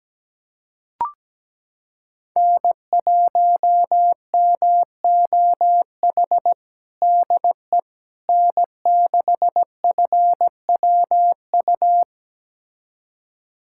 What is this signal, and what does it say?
Morse code, heard as a keyed tone, about 17 wpm, 700 Hz. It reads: N1MOH DE N6FWU
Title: USA Callsign DE Callsign - 5 Characters - 17wpm